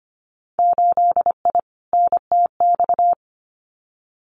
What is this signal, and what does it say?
Morse code, heard as a keyed tone, about 25 wpm, 700 Hz. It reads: 8I NTX
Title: Winter Field Day Exchanges Encore 25wpm